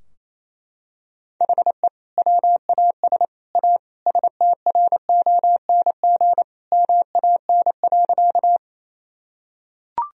Morse code, heard as a keyed tone, about 28 wpm, 700 Hz. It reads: HE WAS A STRONG MAN.
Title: Sentences from Top 1000 Words - Review Part 3 - 28wpm